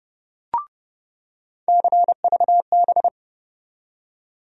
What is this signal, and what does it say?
Morse code, heard as a keyed tone, about 32 wpm, 700 Hz. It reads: CVB
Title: USA Call Sign Postfixes - 3 Letters 30wpm